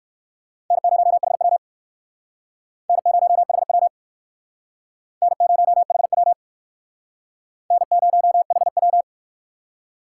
Morse code, heard as a keyed tone, about 58 wpm, 700 Hz. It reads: N0HW N0HW N0HW N0HW